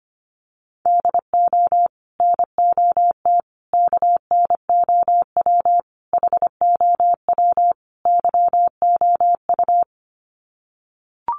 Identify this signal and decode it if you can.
Morse code, heard as a keyed tone, about 25 wpm, 700 Hz. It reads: DO NOT KNOW HOW YOU
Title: Sets of 5 Words Encore 25wpm